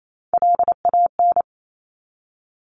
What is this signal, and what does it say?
Morse code, heard as a keyed tone, about 28 wpm, 700 Hz. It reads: LAN
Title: Top Three Letter Combinations in Words 28wpm